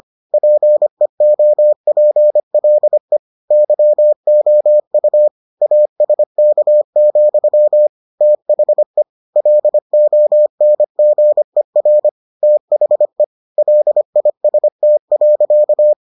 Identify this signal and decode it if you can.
Morse code, heard as a keyed tone, about 25 wpm, 600 Hz. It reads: PEOPLE YOU ASK, THE LONGER THE LIST.